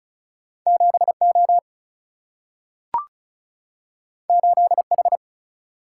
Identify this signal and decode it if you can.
Morse code, heard as a keyed tone, about 36 wpm, 700 Hz. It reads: 7O 8H
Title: Single-Letter Single-Number 35wpm